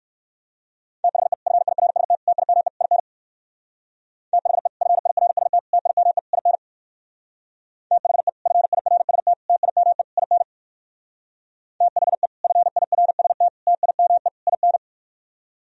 Morse code, heard as a keyed tone, about 55 wpm, 700 Hz. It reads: THE FIRST TIME IN THE FIRST TIME IN THE FIRST TIME IN THE FIRST TIME IN
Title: Speed-Racing - Sets of 4 Words 45wpm